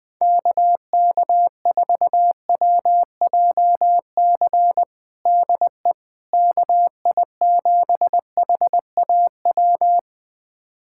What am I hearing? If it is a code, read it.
Morse code, heard as a keyed tone, about 21 wpm, 700 Hz. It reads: KK4WJC DE KI7HAW
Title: USA Callsign DE Callsign - 6 Characters - 20wpm